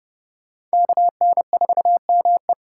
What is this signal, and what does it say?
Morse code, heard as a keyed tone, about 37 wpm, 700 Hz. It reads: KN4ME
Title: Speed Racing - USA 2x2 Call Signs 30wpm